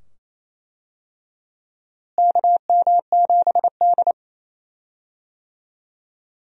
Morse code, heard as a keyed tone, about 27 wpm, 700 Hz. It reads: KM7D
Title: USA Call Signs - 4 Characters - Repeated 2x - 28wpm